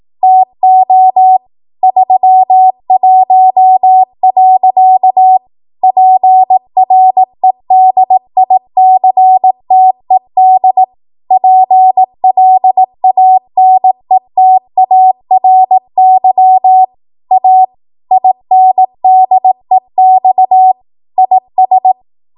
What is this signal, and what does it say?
Morse code, heard as a keyed tone, about 18 wpm, 750 Hz. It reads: TO 31. PREDICTED PLANETARY A INDEX IS
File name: Archive / 18 WPM / 18WPM